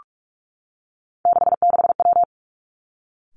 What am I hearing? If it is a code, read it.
Morse code, heard as a keyed tone, about 46 wpm, 700 Hz. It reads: 66W
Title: Three Challenging Letter-Number Combinations 45wpm